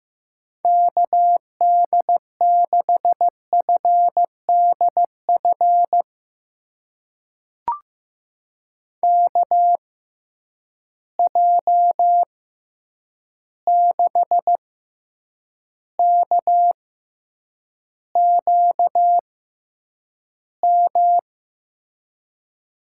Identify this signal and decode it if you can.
Morse code, heard as a keyed tone, about 15 wpm, spaced as for 8 wpm, 700 Hz. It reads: KD6FDF KJ6KQM
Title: USA Call Signs - 6 Characters - 6x Character Spacing - 15wpm